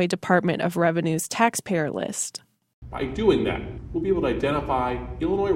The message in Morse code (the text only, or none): none